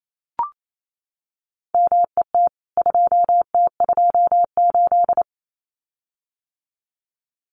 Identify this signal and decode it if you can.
Morse code, heard as a keyed tone, about 28 wpm, 700 Hz. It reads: MET 2T28
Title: CWT Exchange 28wpm